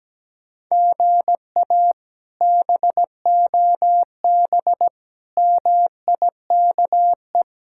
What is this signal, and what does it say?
Morse code, heard as a keyed tone, about 17 wpm, 700 Hz. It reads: GA BOB MIKE
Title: K1USN SST QSO Practice - 17wpm